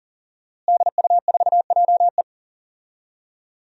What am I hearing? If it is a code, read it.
Morse code, heard as a keyed tone, about 40 wpm, 700 Hz. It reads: DU4JE
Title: DXCC-100 Call Signs - 5 Characters - Repeated 3x - 40wpm